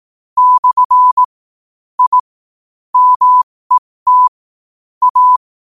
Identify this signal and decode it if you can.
Morse code, heard as a keyed tone, about 18 wpm, spaced as for 15 wpm, 1000 Hz. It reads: / I MET A
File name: qotdfiles / 15wpm